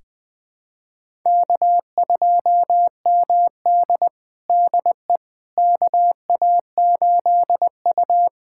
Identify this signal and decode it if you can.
Morse code, heard as a keyed tone, about 20 wpm, 700 Hz. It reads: K2MD DE KA8U
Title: USA Callsign DE Callsign - 4 Characters - 20wpm